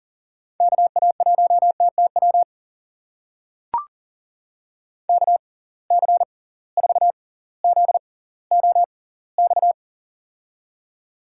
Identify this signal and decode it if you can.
Morse code, heard as a keyed tone, about 40 wpm, spaced as for 20 wpm, 700 Hz. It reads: KA1TTW KC4ZOX
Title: USA Call Signs - 6 Characters - 6x Character Spacing - 40wpm